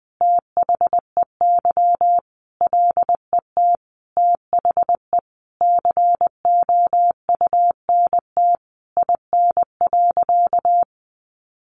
Morse code, heard as a keyed tone, about 20 wpm, 700 Hz. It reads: THEY LET THE COUNT IN.
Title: Sentences from Top 1500 Words 20wpm